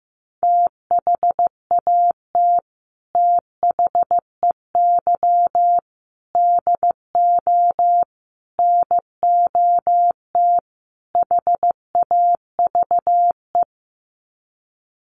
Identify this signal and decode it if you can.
Morse code, heard as a keyed tone, about 15 wpm, 700 Hz. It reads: THAT THEY DO NOT HAVE
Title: Sets of 5 Words Encore2 15wpm